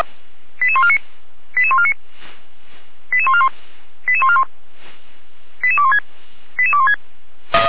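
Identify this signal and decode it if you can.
Five-tone selective call (ZVEI), heard as quick runs of five tones, tones of about 70 ms. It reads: 88149 88149 88141 88141 88216 88216
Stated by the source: ZVEI five-tone selective call received by SDR.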